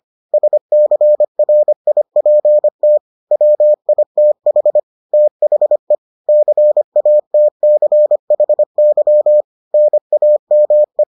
Morse code, heard as a keyed tone, about 25 wpm, 600 Hz. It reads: SCRIPT WITH THE CATCHY NAME